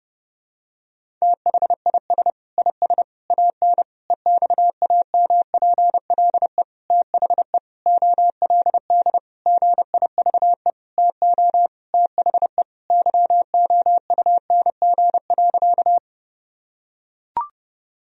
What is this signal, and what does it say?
Morse code, heard as a keyed tone, about 32 wpm, 700 Hz. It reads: THIS IS AN EXAMPLE THE OLD GIVE TO THE YOUNG.
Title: Sentences from Top 1000 Words - Review Part 1 - 30wpm